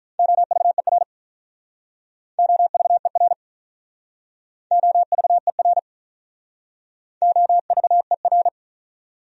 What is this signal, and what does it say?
Morse code, heard as a keyed tone, about 46 wpm, 700 Hz. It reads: OVER OVER OVER OVER